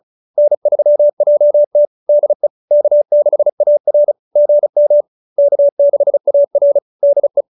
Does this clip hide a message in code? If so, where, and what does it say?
Yes: Morse code from 0.4 to 7.4 s, heard as a keyed tone, about 37 wpm, 570 Hz. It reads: N3JT DE K6AR GM K6AR DE